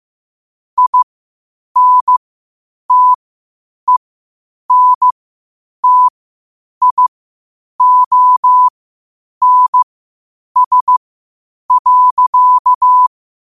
Morse code, heard as a keyed tone, about 15 wpm, spaced as for 8 wpm, 1000 Hz. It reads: INTENTIONS.
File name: qotdfiles / 8wpm